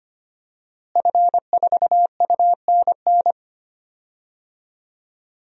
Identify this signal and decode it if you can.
Morse code, heard as a keyed tone, about 25 wpm, 700 Hz. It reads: F4UNN
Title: DXCC-100 Call Signs - 5 Characters - Repeated 2x - 25wpm